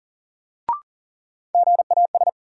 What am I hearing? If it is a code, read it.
Morse code, heard as a keyed tone, about 40 wpm, 700 Hz. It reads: GAS
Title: Sentences from Top 1100 Words - New Word Drill - 40wpm